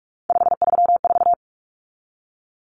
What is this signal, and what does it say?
Morse code, heard as a keyed tone, about 44 wpm, 700 Hz. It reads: 534
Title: Common Cut Numbers - 3-Digit 45wpm